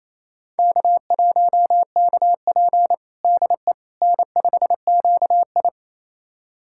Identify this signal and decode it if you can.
Morse code, heard as a keyed tone, about 28 wpm, 700 Hz. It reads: K1KP DE N5QI
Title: USA Callsign DE Callsign - 4 Characters - 28wpm